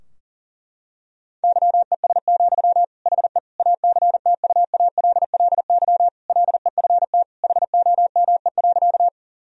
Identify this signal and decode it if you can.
Morse code, heard as a keyed tone, about 39 wpm, 700 Hz. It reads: YES, HE ACTUALLY LEFT HOME.
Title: Sentences from Top 1000 Words - Review Part 4 - 40wpm